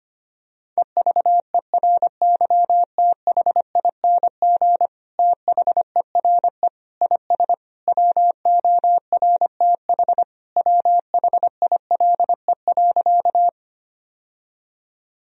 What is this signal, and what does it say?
Morse code, heard as a keyed tone, about 25 wpm, 700 Hz. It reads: EVERYTHING THERE IS WORTH WHILE.
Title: Sentences from Top 1000 Words - Review Part 1 - 25wpm